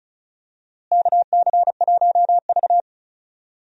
Morse code, heard as a keyed tone, about 36 wpm, 700 Hz. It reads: KC1V
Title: USA Call Signs - 4 Characters - Repeated 2x - 35wpm